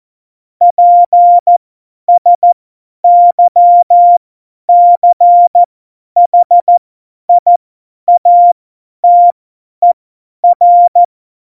Morse code, heard as a keyed tone, about 14 wpm, spaced as for 10 wpm, 700 Hz. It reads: PSYCHIATER
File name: Morsekurs / Witze / Witz 20 March 2018, 50 F70